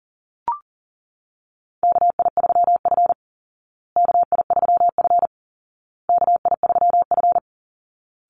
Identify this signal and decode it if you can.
Morse code, heard as a keyed tone, about 37 wpm, 700 Hz. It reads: KI3F KI3F KI3F
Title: Call Signs Repeated 2x - 40wpm